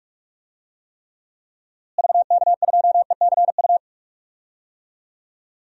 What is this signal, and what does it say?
Morse code, heard as a keyed tone, about 42 wpm, 700 Hz. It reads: VK2ECU
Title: DXCC-100 Call Signs - 6 Characters - Repeated 3x - 45wpm